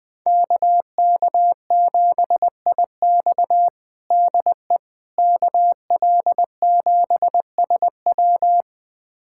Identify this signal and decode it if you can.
Morse code, heard as a keyed tone, about 21 wpm, 700 Hz. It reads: KK7IX DE KL7SW
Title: USA Callsign DE Callsign - 5 Characters - 20wpm